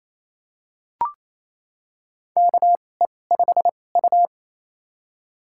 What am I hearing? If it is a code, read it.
Morse code, heard as a keyed tone, about 28 wpm, spaced as for 20 wpm, 700 Hz. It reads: KE5U
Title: USA Call Signs - 4 Characters - 2x Character Spacing - 28wpm